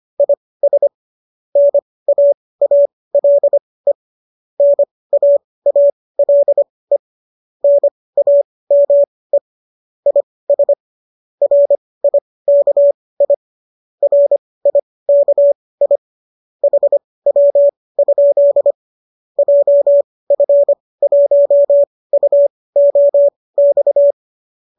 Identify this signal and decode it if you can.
Morse code, heard as a keyed tone, about 25 wpm, spaced as for 18 wpm, 570 Hz. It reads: IS NAALE NAALE NAME IS RIKI RIKI HW? JF1UOX